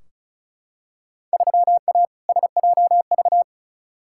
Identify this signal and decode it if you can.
Morse code, heard as a keyed tone, about 36 wpm, 700 Hz. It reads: 3A SJV
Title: Field Day Exchanges Encore - 35wpm